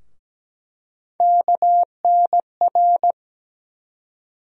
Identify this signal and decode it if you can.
Morse code, heard as a keyed tone, about 17 wpm, 700 Hz. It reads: KNR
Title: USA Call Sign Postfixes - 3 Letters 17wpm